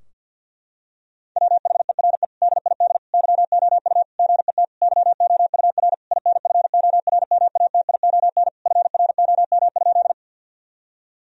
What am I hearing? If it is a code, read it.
Morse code, heard as a keyed tone, about 47 wpm, 700 Hz. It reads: WHERE DID YOU GET YOUR INFORMATION FROM?
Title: Sentences from Top 1000 Words - Review Part 3 - 50wpm